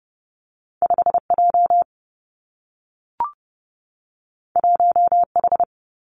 Morse code, heard as a keyed tone, about 30 wpm, 700 Hz. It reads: 5 J 1 H
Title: Two Challenging Letter-Number Combinations 30wpm